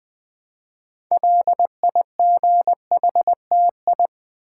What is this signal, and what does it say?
Morse code, heard as a keyed tone, about 20 wpm, 700 Hz. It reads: LIGHTI
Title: Build A Word - 2 Letters At A Time - 8-Letter Words - 20wpm